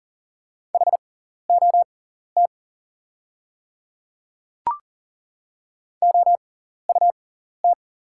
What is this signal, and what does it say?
Morse code, heard as a keyed tone, about 40 wpm, spaced as for 14 wpm, 700 Hz. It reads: HOT OUT
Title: 6x Character Spacing - Top 500 Words - 40wpm